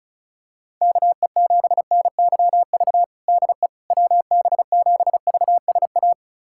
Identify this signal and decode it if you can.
Morse code, heard as a keyed tone, about 36 wpm, 700 Hz. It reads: KE7NYV DE WB7VSA
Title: USA Callsign DE Callsign 35wpm